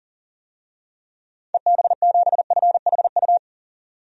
Morse code, heard as a keyed tone, about 37 wpm, 700 Hz. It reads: EB7FHU
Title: DXCC-100 Call Signs - 6 Characters - 40wpm